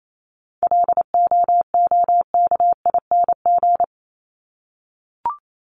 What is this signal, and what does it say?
Morse code, heard as a keyed tone, about 28 wpm, 700 Hz. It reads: LOOKING
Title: ING Suffix - Top 100 Words with Base Words 28wpm